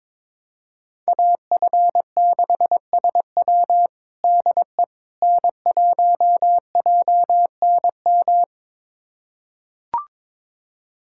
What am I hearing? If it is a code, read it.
Morse code, heard as a keyed tone, about 21 wpm, 700 Hz. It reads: AF6SW DE N1JNM
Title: USA Callsign DE Callsign - 5 Characters - 22wpm